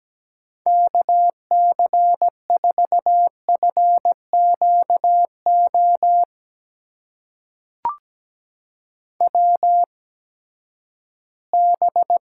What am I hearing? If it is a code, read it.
Morse code, heard as a keyed tone, about 17 wpm, spaced as for 12 wpm, 700 Hz. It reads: KC4FQO WB
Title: USA Call Signs - 6 Characters - 8x Character Spacing - 17wpm